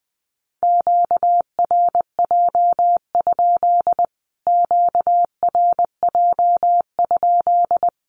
Morse code, heard as a keyed tone, about 20 wpm, 700 Hz. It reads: QRJ? QRJ?
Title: Q-Codes Repeated 3x 20wpm